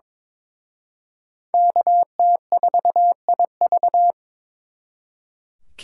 Morse code, heard as a keyed tone, about 26 wpm, 700 Hz. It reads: KT4IV
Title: Speed Racing - USA 2x2 Call Signs 22wpm